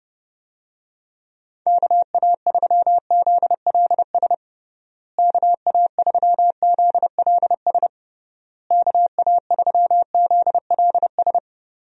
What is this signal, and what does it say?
Morse code, heard as a keyed tone, about 32 wpm, 700 Hz. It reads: KA3ZLS KA3ZLS KA3ZLS
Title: USA Call Signs - 6 Characters - Repeated 3x - 30wpm